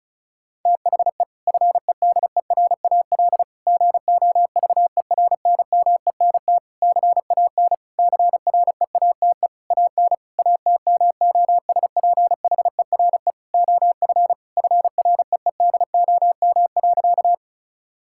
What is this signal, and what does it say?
Morse code, heard as a keyed tone, about 36 wpm, 700 Hz. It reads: THE FEDERAL GOVERNMENT CAN CREATE AN ATMOSPHERE OF FREEDOM.
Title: Sentences from Top 1600 Words 35wpm